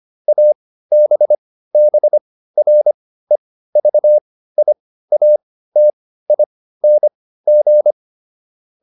Morse code, heard as a keyed tone, about 25 wpm, spaced as for 15 wpm, 600 Hz. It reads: ABBREVIATING